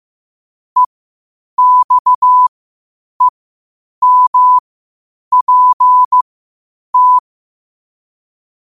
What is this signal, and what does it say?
Morse code, heard as a keyed tone, about 15 wpm, spaced as for 8 wpm, 1000 Hz. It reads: EXEMPT
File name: qotdfiles / 8wpm